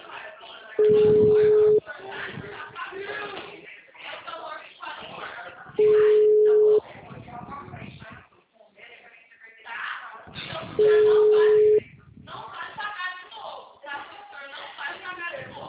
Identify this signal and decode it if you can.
Ringback tone (Europe, 425 Hz), with speech.